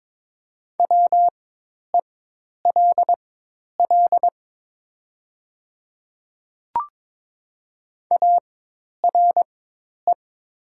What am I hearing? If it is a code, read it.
Morse code, heard as a keyed tone, about 22 wpm, spaced as for 10 wpm, 700 Hz. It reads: WELL ARE
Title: 4x Character Spacing - Top 300 Words - 22wpm